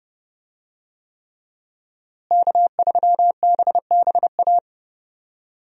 Morse code, heard as a keyed tone, about 31 wpm, 700 Hz. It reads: K3BBA
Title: USA Call Signs - 5 Characters - Repeated 3x - 30wpm